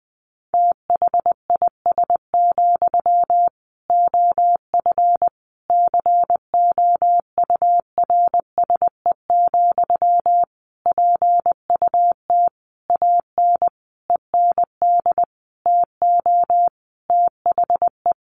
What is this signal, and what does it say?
Morse code, heard as a keyed tone, about 20 wpm, 700 Hz. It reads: THIS, OF COURSE, PUT AN END TO THE
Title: Sentences from Top 1400 Words 20wpm